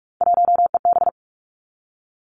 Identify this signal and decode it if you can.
Morse code, heard as a keyed tone, about 46 wpm, 700 Hz. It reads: 1EB
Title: Three Challenging Letter-Number Combinations 45wpm